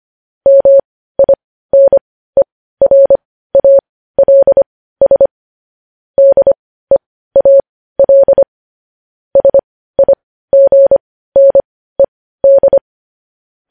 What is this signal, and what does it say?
Morse code, heard as a keyed tone, about 25 wpm, spaced as for 15 wpm, 550 Hz. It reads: MINERALS DEAL SIGNED
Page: news headlines in morse code at 15 wpm